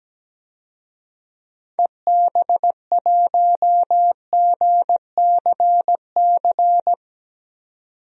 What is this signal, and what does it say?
Morse code, heard as a keyed tone, about 17 wpm, 700 Hz. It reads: EB1GCC
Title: DXCC-100 Call Signs - 6 Characters - 17wpm